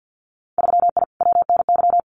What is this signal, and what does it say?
Morse code, heard as a keyed tone, about 50 wpm, 700 Hz. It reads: 3I WNY
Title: Winter Field Day Exchanges 50wpm